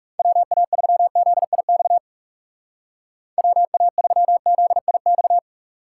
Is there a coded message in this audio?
Yes: Morse code from 0.2 to 5.4 s, heard as a keyed tone, about 40 wpm, 700 Hz. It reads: WA3ZIX WA3ZIX